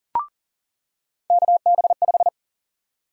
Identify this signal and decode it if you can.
Morse code, heard as a keyed tone, about 40 wpm, 700 Hz. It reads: KB5
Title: USA Call Sign Prefixes - 40wpm